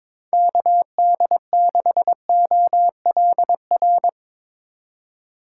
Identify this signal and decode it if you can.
Morse code, heard as a keyed tone, about 24 wpm, 700 Hz. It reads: KD6OLR
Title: Speed Racing - USA 2x3 Call Signs 22wpm